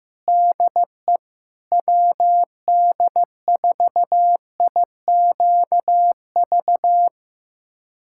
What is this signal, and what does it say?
Morse code, heard as a keyed tone, about 16 wpm, 700 Hz. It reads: DE WD4IQV
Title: USA Callsign DE Callsign - 6 Characters - 15wpm